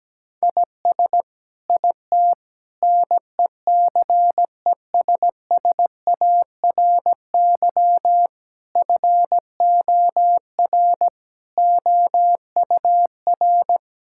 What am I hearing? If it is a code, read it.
Morse code, heard as a keyed tone, about 17 wpm, 700 Hz. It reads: IS IT NECESSARY FOR OUR
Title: Sentences from Top 1000 Words - Review Part 6 - 17wpm